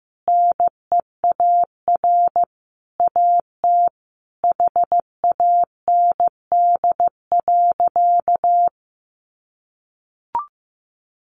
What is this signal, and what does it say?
Morse code, heard as a keyed tone, about 15 wpm, 700 Hz. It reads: NEAR AT HAND.
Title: Sentences from Top 1500 Words 15wpm